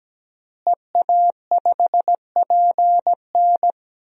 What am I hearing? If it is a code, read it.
Morse code, heard as a keyed tone, about 17 wpm, 700 Hz. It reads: EA5PN
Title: DXCC-100 Call Signs - 5 Characters - Repeated 2x - 17wpm